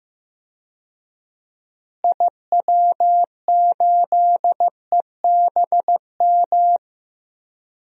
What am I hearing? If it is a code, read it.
Morse code, heard as a keyed tone, about 15 wpm, 700 Hz. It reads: IW8EBM
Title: DXCC-100 Call Signs - 6 Characters - Repeated 3x - 15wpm